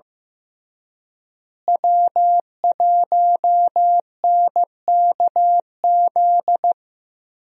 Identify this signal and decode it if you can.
Morse code, heard as a keyed tone, about 15 wpm, 700 Hz. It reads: W1NKZ
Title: USA Call Signs - 5 Characters - Repeated 3x - 15wpm